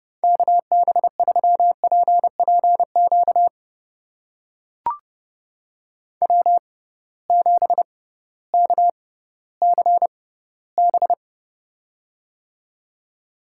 Morse code, heard as a keyed tone, about 31 wpm, spaced as for 16 wpm, 700 Hz. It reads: KB3PPQ W7KCB
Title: USA Call Signs - 6x Character Spacing - 30wpm